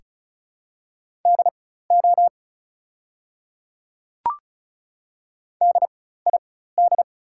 Morse code, heard as a keyed tone, about 34 wpm, spaced as for 16 wpm, 700 Hz. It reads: DO DID
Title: 4x Character Spacing - Top 100 Words - 35wpm